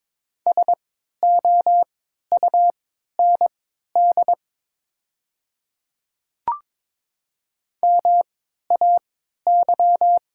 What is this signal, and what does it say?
Morse code, heard as a keyed tone, about 22 wpm, spaced as for 12 wpm, 700 Hz. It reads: SOUND MAY
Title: 3x Character Spacing - Top 100 Words - 22wpm